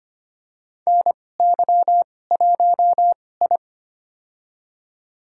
Morse code, heard as a keyed tone, about 25 wpm, spaced as for 18 wpm, 700 Hz. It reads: NY1I
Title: USA Call Signs - 2x Character Spacing - 25wpm